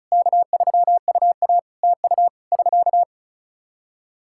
Morse code, heard as a keyed tone, about 36 wpm, 700 Hz. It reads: K3UA TU <SK>
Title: Practice QSO - Exchange 4 - Ending - 35wpm